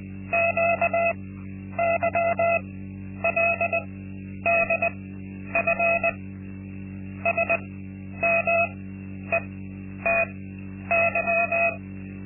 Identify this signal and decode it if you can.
Morse code, heard as a keyed tone, about 20 wpm, spaced as for 10 wpm, 680 Hz. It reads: QYLDF SMETY
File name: Linksmakalnis-2